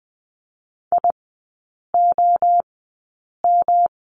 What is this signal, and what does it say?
Morse code, heard as a keyed tone, about 19 wpm, spaced as for 8 wpm, 690 Hz. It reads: IOM
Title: 145 Lesson 24e - Letter y - Three Characters 20wpm